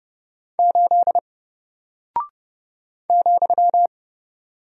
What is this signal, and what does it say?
Morse code, heard as a keyed tone, about 30 wpm, 700 Hz. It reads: 8 ,
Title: Single Character (Letter:Number:Punctuation:Prosign) Alt 30wpm